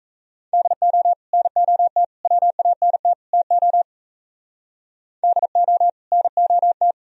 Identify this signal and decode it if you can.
Morse code, heard as a keyed tone, about 35 wpm, 700 Hz. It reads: DO NOT WANT TO DO NOT
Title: Speed-Racing - Sets of 4 Words 28wpm